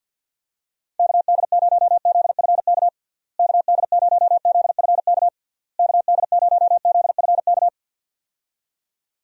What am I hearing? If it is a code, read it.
Morse code, heard as a keyed tone, about 52 wpm, 700 Hz. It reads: KD0ZFK KD0ZFK KD0ZFK
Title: USA Call Signs - 6 Characters - Repeated 3x - 50wpm